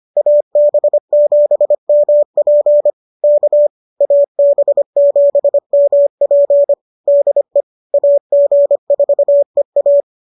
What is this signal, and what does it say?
Morse code, heard as a keyed tone, about 25 wpm, 590 Hz. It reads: AB7MP K AB7MP DE AG4EA
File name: qso206 25